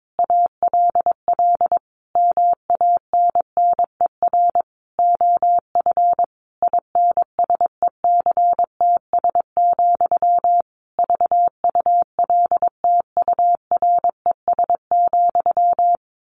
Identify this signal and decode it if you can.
Morse code, heard as a keyed tone, about 22 wpm, 700 Hz. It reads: ALL MANNER OF INSECTS, VULTURES,
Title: A Sound Of Thunder 22wpm